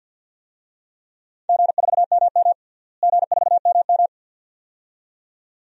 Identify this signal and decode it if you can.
Morse code, heard as a keyed tone, about 49 wpm, 700 Hz. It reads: G4MM G4MM
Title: DXCC-100 Call Signs - 4 Characters - Repeated 2x - 50wpm